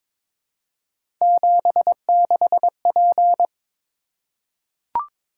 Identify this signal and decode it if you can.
Morse code, heard as a keyed tone, about 22 wpm, 700 Hz. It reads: 76P
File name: practice / 22wpm / Three Letter-Number Combinations - 22wpm